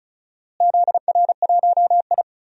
Morse code, heard as a keyed tone, about 34 wpm, 700 Hz. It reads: ZR1I
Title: DXCC-100 Call Signs - 4 Characters - Repeated 3x - 35wpm